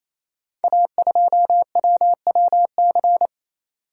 Morse code, heard as a keyed tone, about 29 wpm, 700 Hz. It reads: A2WWC
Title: Code Groups - 5 Letter:Numbers - 28wpm